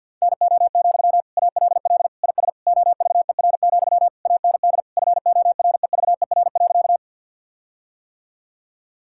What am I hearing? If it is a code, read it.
Morse code, heard as a keyed tone, about 50 wpm, 700 Hz. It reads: NO, ALL IS OVER, AND FOREVER.
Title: Sentences from Top 1600 Words 50wpm